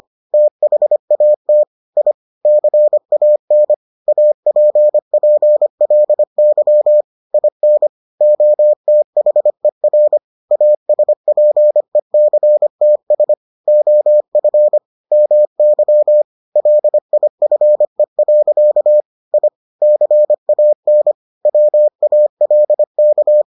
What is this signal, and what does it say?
Morse code, heard as a keyed tone, about 25 wpm, 600 Hz. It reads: THAT I CAN APPLY IN OTHER ASPECTS OF MY LIFE. I CAN WALK